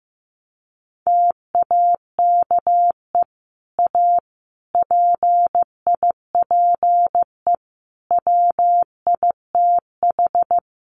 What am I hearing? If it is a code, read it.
Morse code, heard as a keyed tone, about 14 wpm, 700 Hz. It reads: TAKE A PIPE WITH
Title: Sentences from Top 1500 Words 15wpm